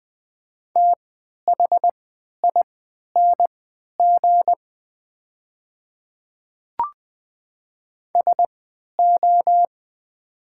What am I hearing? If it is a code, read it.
Morse code, heard as a keyed tone, about 20 wpm, spaced as for 11 wpm, 700 Hz. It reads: THING SO
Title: 3x Character Spacing - Top 100 Words - 20wpm